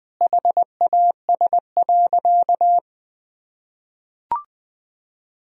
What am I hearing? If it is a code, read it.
Morse code, heard as a keyed tone, about 20 wpm, 700 Hz. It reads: HAS.
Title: Sentences from Top 500 Words and Abbreviations 20wpm